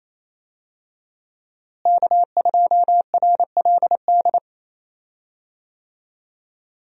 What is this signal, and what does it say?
Morse code, heard as a keyed tone, about 28 wpm, 700 Hz. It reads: K2RLD
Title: USA Call Signs - 5 Characters - Repeated 2x - 28wpm